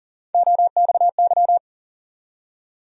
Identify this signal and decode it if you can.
Morse code, heard as a keyed tone, about 40 wpm, 700 Hz. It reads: OXY
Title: USA Call Sign Postfixes - 3 Letters 40wpm